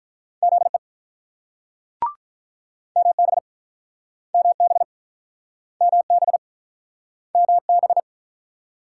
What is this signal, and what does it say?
Morse code, heard as a keyed tone, about 46 wpm, 700 Hz. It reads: ZE MB MB MB MB